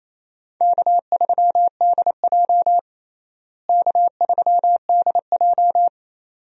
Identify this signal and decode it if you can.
Morse code, heard as a keyed tone, about 28 wpm, 700 Hz. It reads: K3DJ K3DJ
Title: USA Call Signs - 4 Characters - Repeated 2x - 28wpm